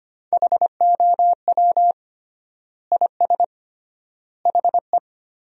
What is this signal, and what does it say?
Morse code, heard as a keyed tone, about 25 wpm, 700 Hz. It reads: HOW IS HE
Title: 3x Word-Spacing - Sentences from Top 100 Words - 25wpm